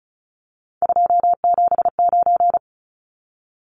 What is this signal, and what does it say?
Morse code, heard as a keyed tone, about 34 wpm, 700 Hz. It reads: 279
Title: Numbers - 3-Digit 35wpm